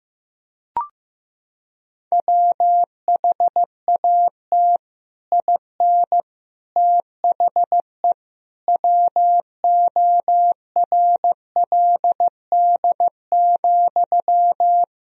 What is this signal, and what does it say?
Morse code, heard as a keyed tone, about 16 wpm, 700 Hz. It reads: WHAT IN THE WORLD,
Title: Sentences from Top 1000 Words - Review Part 1 - 15wpm